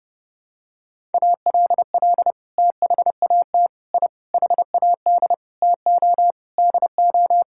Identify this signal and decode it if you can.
Morse code, heard as a keyed tone, about 30 wpm, 700 Hz. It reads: ALL THAT I HAD TO DO
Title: Sentences from Top 1000 Words - Review Part 6 - 30wpm